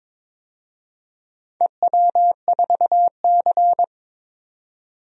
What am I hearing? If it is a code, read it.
Morse code, heard as a keyed tone, about 22 wpm, 700 Hz. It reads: EW4C
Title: DXCC-100 Call Signs - 4 Characters - 22wpm